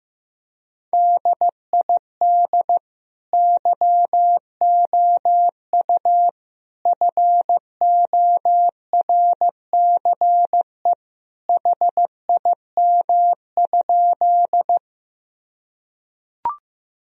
Morse code, heard as a keyed tone, about 16 wpm, 700 Hz. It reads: DID YOU FORCE HIM?
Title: Sentences from Top 1000 Words - Review Part 3 - 15wpm